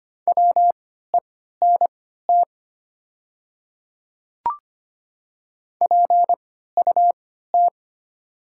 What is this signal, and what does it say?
Morse code, heard as a keyed tone, about 25 wpm, spaced as for 14 wpm, 700 Hz. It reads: WENT PUT
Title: 3x Character Spacing - Top 200 Words - 25wpm